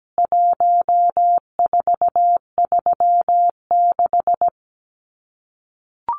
Morse code, heard as a keyed tone, about 17 wpm, 700 Hz. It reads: 1436
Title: CWT Exchange 17wpm